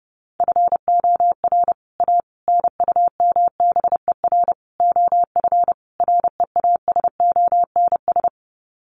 Morse code, heard as a keyed tone, about 30 wpm, 700 Hz. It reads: FOR A NUMBER OF REASONS
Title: Sets of 5 Words Encore2 30wpm